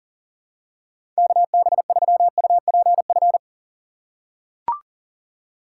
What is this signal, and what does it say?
Morse code, heard as a keyed tone, about 40 wpm, 700 Hz. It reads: KB3UPF
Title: USA Call Signs - 6 Characters - 3x Character Spacing - 40wpm